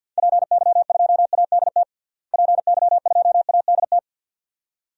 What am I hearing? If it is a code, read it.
Morse code, heard as a keyed tone, about 50 wpm, 700 Hz. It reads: PY2ADT PY2ADT
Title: DXCC-100 Call Signs - 6 Characters - Repeated 2x - 50wpm